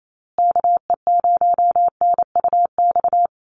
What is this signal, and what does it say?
Morse code, heard as a keyed tone, about 28 wpm, 700 Hz. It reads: KE0NUX
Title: USA Call Signs Repeated 3x 28wpm